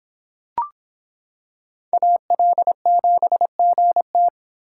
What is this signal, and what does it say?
Morse code, heard as a keyed tone, about 26 wpm, 700 Hz. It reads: AL7GT